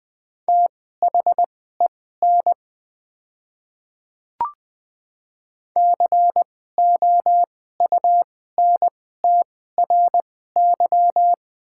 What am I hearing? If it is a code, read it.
Morse code, heard as a keyed tone, about 20 wpm, spaced as for 14 wpm, 690 Hz. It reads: THEN COUNTRY
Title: 2x Character Spacing - Top 300 Words - 20wpm